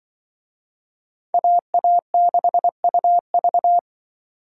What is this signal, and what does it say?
Morse code, heard as a keyed tone, about 24 wpm, 700 Hz. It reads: AA6UV
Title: Speed Racing - USA 2x2 Call Signs 20wpm